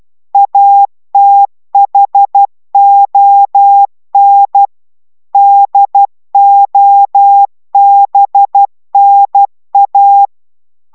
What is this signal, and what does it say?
Morse code, heard as a keyed tone, about 12 wpm, 800 Hz. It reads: ATHON DOBNA